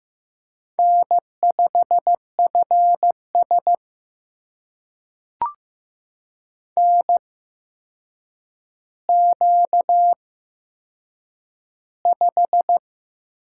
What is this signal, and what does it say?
Morse code, heard as a keyed tone, about 15 wpm, spaced as for 7 wpm, 700 Hz. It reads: N5FS NQ5
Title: USA Call Signs - 4 Characters - 8x Character Spacing - 15wpm